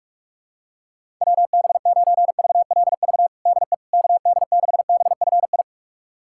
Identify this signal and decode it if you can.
Morse code, heard as a keyed tone, about 46 wpm, 700 Hz. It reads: WB9VLV DE KD6BFI
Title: USA Callsign DE Callsign 45wpm